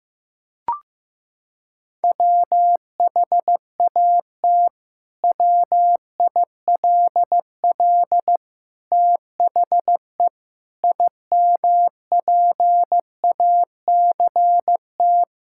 Morse code, heard as a keyed tone, about 15 wpm, 700 Hz. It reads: WHAT WILL THE IMPACT
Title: Sentences from Top 1000 Words - Review Part 6 - 15wpm